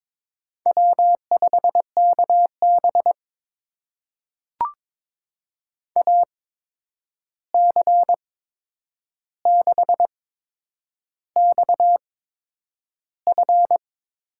Morse code, heard as a keyed tone, about 22 wpm, spaced as for 9 wpm, 700 Hz. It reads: W5KB AC6XF
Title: USA Call Signs - 8x Character Spacing - 22wpm